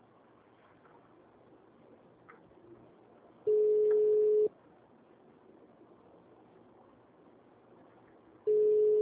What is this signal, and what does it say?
Ringback tone (Europe, 425 Hz).